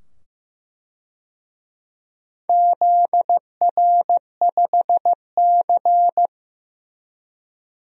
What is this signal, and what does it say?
Morse code, heard as a keyed tone, about 15 wpm, 700 Hz. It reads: ZR5C
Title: DXCC-100 Call Signs - 4 Characters - 15wpm